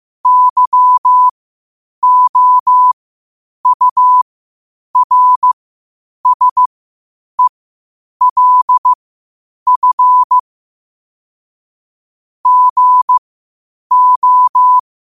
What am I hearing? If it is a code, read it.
Morse code, heard as a keyed tone, about 15 wpm, spaced as for 8 wpm, 1000 Hz. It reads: YOURSELF GO
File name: qotdfiles / 8wpm